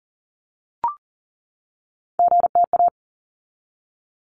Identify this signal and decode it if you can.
Morse code, heard as a keyed tone, about 40 wpm, 700 Hz. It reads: GTA
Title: ARRL Sections 40wpm